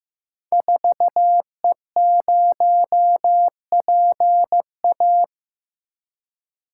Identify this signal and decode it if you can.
Morse code, heard as a keyed tone, about 15 wpm, 700 Hz. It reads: 4E0PA
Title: Code Groups - 5 Letter:Numbers - 15wpm